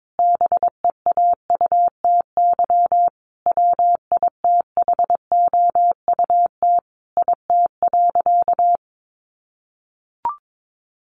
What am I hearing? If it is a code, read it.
Morse code, heard as a keyed tone, about 22 wpm, 700 Hz. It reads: BEAUTY WITHOUT IT.
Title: Sentences from Top 600 Words 22wpm